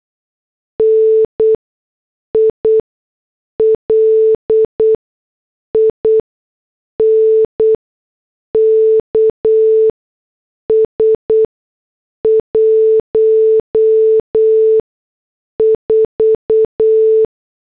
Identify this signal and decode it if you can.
Morse code, heard as a keyed tone, about 8 wpm, spaced as for 6 wpm, 440 Hz. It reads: NI LINKS 14
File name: morse / headlines